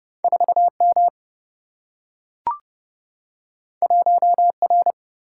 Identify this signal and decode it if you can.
Morse code, heard as a keyed tone, about 30 wpm, 700 Hz. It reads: 4 M 1 R
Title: Single-Letter Single-Number 30wpm